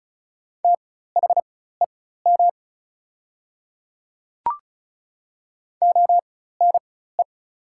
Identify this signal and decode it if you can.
Morse code, heard as a keyed tone, about 36 wpm, spaced as for 16 wpm, 700 Hz. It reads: THEM ONE
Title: 4x Character Spacing - Top 300 Words - 35wpm